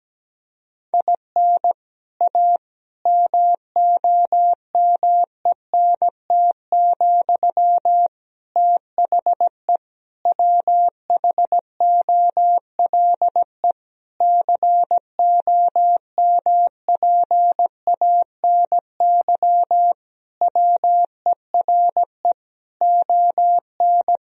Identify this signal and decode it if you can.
Morse code, heard as a keyed tone, about 17 wpm, 700 Hz. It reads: IN A MOMENT, THE WHOLE COMPANY WERE ON
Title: Sentences from Top 1000 Words - Review Part 3 - 17wpm